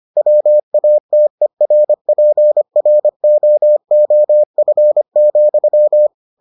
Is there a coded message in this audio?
Yes: Morse code from 0.2 to 6.1 s, heard as a keyed tone, about 25 wpm, 600 Hz. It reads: WATERPROOF,